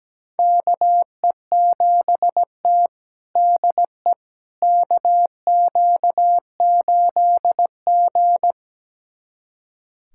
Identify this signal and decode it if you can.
Morse code, heard as a keyed tone, about 17 wpm, 700 Hz. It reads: KE7T DE KQ8G
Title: USA Callsign DE Callsign - 4 Characters - 17wpm